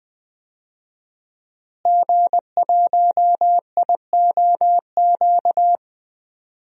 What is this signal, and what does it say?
Morse code, heard as a keyed tone, about 20 wpm, 700 Hz. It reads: G1IOQ
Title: DXCC-100 Call Signs - 5 Characters - Repeated 3x - 20wpm